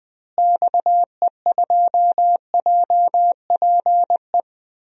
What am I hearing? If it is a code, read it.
Morse code, heard as a keyed tone, about 20 wpm, 700 Hz. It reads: XE2JPE
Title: DXCC-100 Call Signs - 6 Characters - Repeated 2x - 20wpm